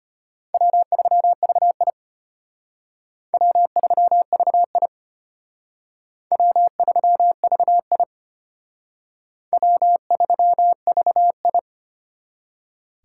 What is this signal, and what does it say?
Morse code, heard as a keyed tone, about 32 wpm, 700 Hz. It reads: W3VI W3VI W3VI W3VI